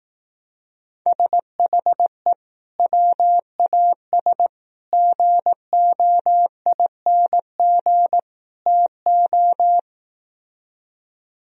Morse code, heard as a keyed tone, about 18 wpm, 700 Hz. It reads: SHE WAS GOING TO
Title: Speed-Racing - Sets of 4 Words 15wpm